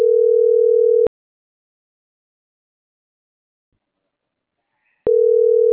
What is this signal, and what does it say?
Ringback tone (North America), 440 + 480 Hz.
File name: CENTRAL SAGOT / out adriana.poncec - .77766